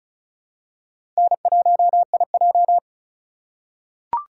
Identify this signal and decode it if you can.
Morse code, heard as a keyed tone, about 36 wpm, 700 Hz. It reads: N1IJ
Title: USA Call Signs - 4 Characters - 3x Character Spacing - 35wpm